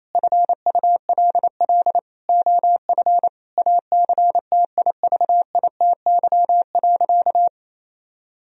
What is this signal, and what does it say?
Morse code, heard as a keyed tone, about 28 wpm, 700 Hz. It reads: FULL OF ACTIVITY.
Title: Sentences from Top 1000 Words - Review Part 4 - 28wpm